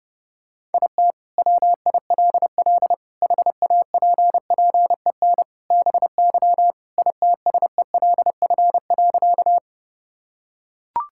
Morse code, heard as a keyed tone, about 30 wpm, 700 Hz. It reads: IT WILL HAPPEN BY ITSELF.
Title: Sentences from Top 1000 Words - Review Part 5 - 30wpm